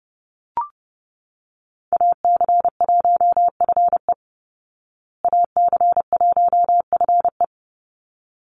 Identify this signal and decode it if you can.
Morse code, heard as a keyed tone, about 28 wpm, 700 Hz. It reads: AC1FE AC1FE
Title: Call Signs Repeated 3x - 30wpm